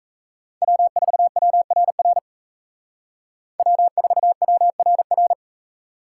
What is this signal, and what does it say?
Morse code, heard as a keyed tone, about 36 wpm, 700 Hz. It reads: W4WRR W4WRR